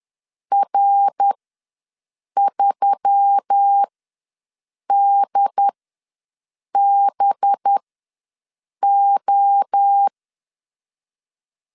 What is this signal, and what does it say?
Morse code, heard as a keyed tone, about 10 wpm, spaced as for 6 wpm, 800 Hz. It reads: R3DBO
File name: LEZIONI / 5 LBVC / 5 LB ws 10wpm 3616